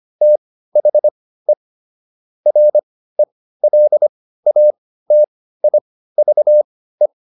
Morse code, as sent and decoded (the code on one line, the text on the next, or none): - .... / .-. .-.. .- - .. ...- .
THE RELATIVE